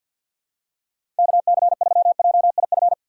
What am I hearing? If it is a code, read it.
Morse code, heard as a keyed tone, about 47 wpm, 700 Hz. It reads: K C 3 J I F
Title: USA Call Signs - 6 Characters - 6x Character Spacing - 50wpm